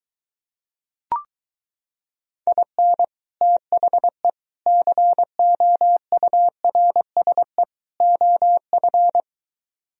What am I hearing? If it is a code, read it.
Morse code, heard as a keyed tone, about 23 wpm, 700 Hz. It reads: IN THE COURSE OF